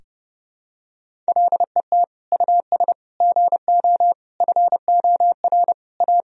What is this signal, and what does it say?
Morse code, heard as a keyed tone, about 30 wpm, 700 Hz. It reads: LET US GO FOR A
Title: Sentences from Top 1000 Words - Review Part 1 - 30wpm